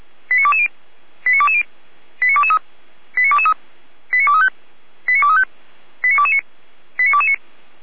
Five-tone selective call (ZVEI), heard as quick runs of five tones, tones of about 70 ms. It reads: 89220 89220 89223 89223 89235 89235 89229 89229